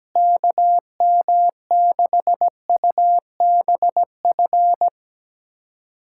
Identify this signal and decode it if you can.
Morse code, heard as a keyed tone, about 17 wpm, 700 Hz. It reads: KM6UBF
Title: USA Callsign DE Callsign 17wpm